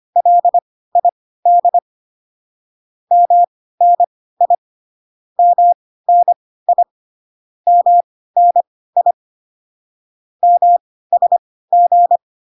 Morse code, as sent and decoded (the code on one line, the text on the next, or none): .-.. .. -.. / -- -. .. / -- -. .. / -- -. .. / -- ... --.
LID MNI MNI MNI MSG